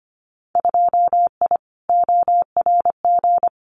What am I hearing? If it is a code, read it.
Morse code, heard as a keyed tone, about 25 wpm, 700 Hz. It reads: 2I ORG
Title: Winter Field Day Exchanges Encore 25wpm